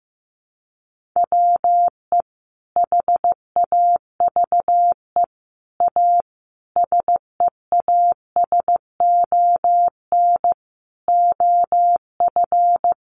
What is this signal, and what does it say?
Morse code, heard as a keyed tone, about 15 wpm, 700 Hz. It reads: WE HAVE A SEASON OF
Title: Sentences from Top 1400 Words 15wpm